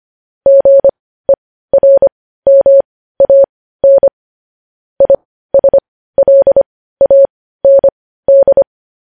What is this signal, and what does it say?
Morse code, heard as a keyed tone, about 25 wpm, spaced as for 15 wpm, 550 Hz. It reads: GERMAN ISLAND